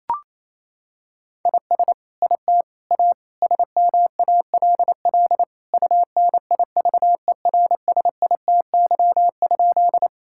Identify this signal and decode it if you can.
Morse code, heard as a keyed tone, about 28 wpm, 700 Hz. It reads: IS IT A SMALL UNIVERSITY?
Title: Sentences from Top 1000 Words - Review Part 5 - 28wpm